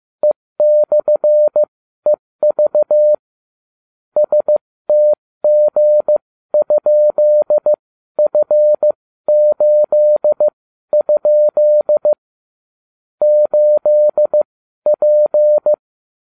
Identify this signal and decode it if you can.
Morse code, heard as a keyed tone, about 15 wpm, spaced as for 12 wpm, 600 Hz. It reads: E/EV STG?F8? 8P